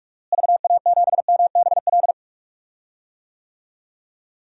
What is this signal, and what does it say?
Morse code, heard as a keyed tone, about 45 wpm, 700 Hz. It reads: VA7MBL
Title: DXCC-100 Call Signs - 6 Characters - Repeated 2x - 45wpm